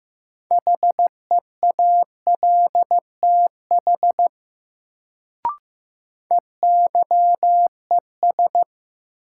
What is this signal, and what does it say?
Morse code, heard as a keyed tone, about 15 wpm, 700 Hz. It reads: HEALTH EYES
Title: Sets of 5 Words - New Word Drill - 15wpm